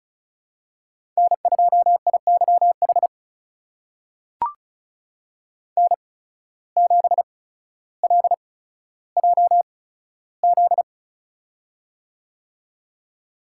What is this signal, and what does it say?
Morse code, heard as a keyed tone, about 34 wpm, spaced as for 15 wpm, 700 Hz. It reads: N2IYH N7LJZ
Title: USA Call Signs - 5 Characters - 8x Character Spacing - 35wpm